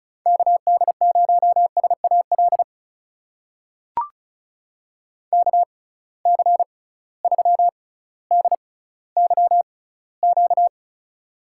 Morse code, heard as a keyed tone, about 36 wpm, spaced as for 18 wpm, 700 Hz. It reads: KD0SAL KC3DYQ
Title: USA Call Signs - 6 Characters - 6x Character Spacing - 35wpm